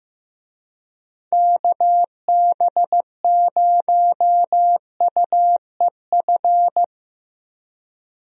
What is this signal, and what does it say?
Morse code, heard as a keyed tone, about 15 wpm, 700 Hz. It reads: KB0UEF
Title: USA Call Signs - 6 Characters - Repeated 2x - 15wpm